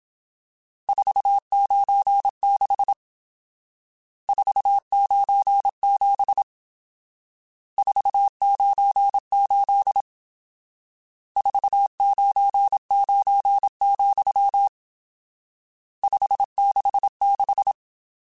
Morse code, heard as a keyed tone, about 27 wpm, 800 Hz. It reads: 496 497 498 499, 566
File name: RSTs / RST-mp3s / Various RST @25